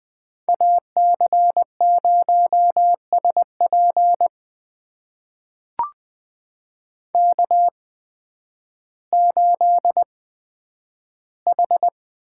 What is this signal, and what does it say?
Morse code, heard as a keyed tone, about 20 wpm, spaced as for 11 wpm, 700 Hz. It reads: AC0SP K8H
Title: USA Call Signs - 8x Character Spacing - 20wpm